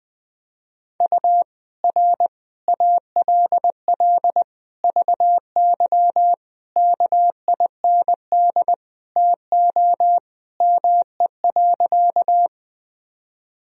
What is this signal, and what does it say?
Morse code, heard as a keyed tone, about 20 wpm, 700 Hz. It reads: U R ALL VY KIND TO ME.
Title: Sentences from Top 200 Words and Abbreviations 20wpm